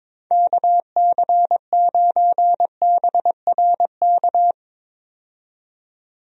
Morse code, heard as a keyed tone, about 21 wpm, 700 Hz. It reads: KC9BRK
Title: USA Call Signs - 6 Characters - 22wpm